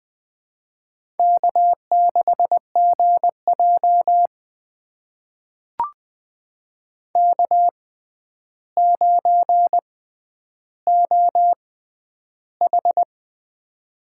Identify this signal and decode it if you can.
Morse code, heard as a keyed tone, about 20 wpm, spaced as for 10 wpm, 700 Hz. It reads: K6GJ K9OH
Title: USA Call Signs - 4 Characters - 6x Character Spacing - 20wpm